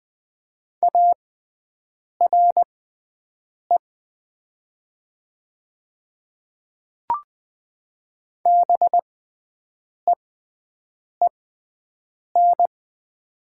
Morse code, heard as a keyed tone, about 20 wpm, spaced as for 7 wpm, 700 Hz. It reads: ARE BEEN
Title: 6x Character Spacing - Top 100 Words - 20wpm